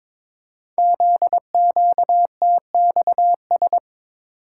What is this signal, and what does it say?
Morse code, heard as a keyed tone, about 22 wpm, 700 Hz. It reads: ZQTXS
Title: Code Groups - 5 Letters - 22wpm